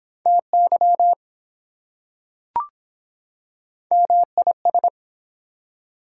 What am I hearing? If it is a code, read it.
Morse code, heard as a keyed tone, about 23 wpm, 700 Hz. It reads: TY MIS